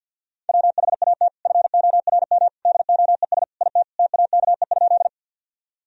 Morse code, heard as a keyed tone, about 52 wpm, 700 Hz. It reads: WHAT FORM DOES IT TAKE?
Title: Sentences from Top 1000 Words - Review Part 1 - 50wpm